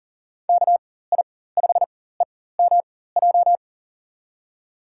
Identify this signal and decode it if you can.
Morse code, heard as a keyed tone, about 40 wpm, spaced as for 19 wpm, 700 Hz. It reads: KI5EMJ
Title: USA Call Signs - 6 Characters - 4x Character Spacing - 40wpm